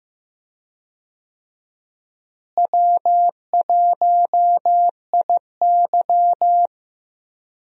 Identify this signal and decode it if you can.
Morse code, heard as a keyed tone, about 16 wpm, 700 Hz. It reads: W1IY
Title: USA Call Signs - 4 Characters - Repeated 2x - 15wpm